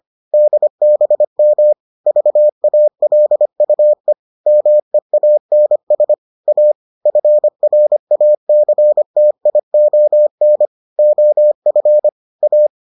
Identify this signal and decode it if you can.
Morse code, heard as a keyed tone, about 25 wpm, 600 Hz. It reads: DBM VALUE MEANS A FRACTION OF A